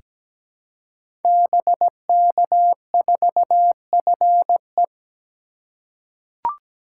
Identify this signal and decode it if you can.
Morse code, heard as a keyed tone, about 17 wpm, 700 Hz. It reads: BK4FE
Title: Code Groups - 5 Letter:Numbers - 17wpm